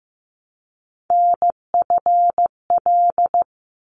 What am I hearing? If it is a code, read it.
Morse code, heard as a keyed tone, about 15 wpm, 700 Hz. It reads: NFL
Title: ARRL Sections 15wpm